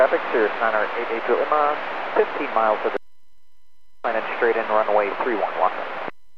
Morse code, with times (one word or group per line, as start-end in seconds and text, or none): none